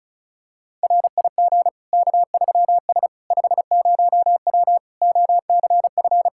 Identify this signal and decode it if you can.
Morse code, heard as a keyed tone, about 36 wpm, 700 Hz. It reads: RIG K3S 50W OCF
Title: K1USN SST QSO Practice 35wpm